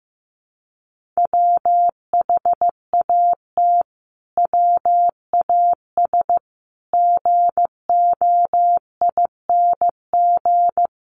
Morse code, heard as a keyed tone, about 15 wpm, 700 Hz. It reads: WHAT WAS GOING
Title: Sets of 5 Words Encore 15wpm